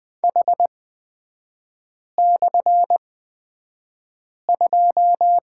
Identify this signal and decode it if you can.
Morse code, heard as a keyed tone, about 20 wpm, spaced as for 5 wpm, 690 Hz. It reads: H/2
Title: 229 Lesson 37e - Letter x - Three Characters 20wpm